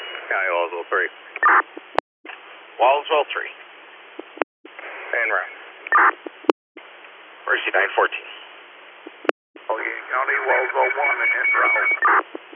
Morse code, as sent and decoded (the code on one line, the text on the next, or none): none